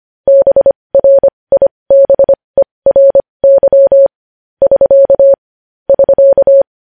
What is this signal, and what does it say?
Morse code, heard as a keyed tone, about 25 wpm, spaced as for 20 wpm, 550 Hz. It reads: BRIBERY <SK> <SK>
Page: news-headlines-in-morse-code-at-20-wpm